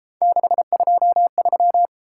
Morse code, heard as a keyed tone, about 32 wpm, 700 Hz. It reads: 623